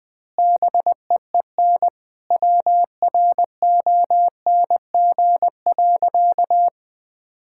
Morse code, heard as a keyed tone, about 20 wpm, 700 Hz. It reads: BEEN WRONG.
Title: Sentences with 9-Word Phrases - Spoken Beginning - 20wpm